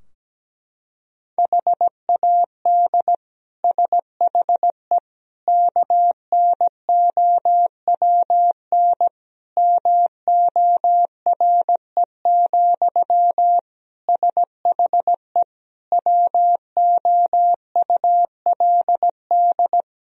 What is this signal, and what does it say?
Morse code, heard as a keyed tone, about 17 wpm, 700 Hz. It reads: HAD SHE KNOWN MORE, SHE WOULD
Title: Sentences from Top 1000 Words - Review Part 3 - 17wpm